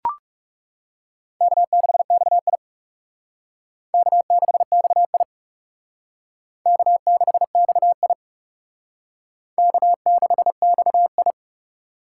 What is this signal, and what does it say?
Morse code, heard as a keyed tone, about 38 wpm, 700 Hz. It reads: K6XI K6XI K6XI K6XI
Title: Speed Racing - USA Call Signs 30wpm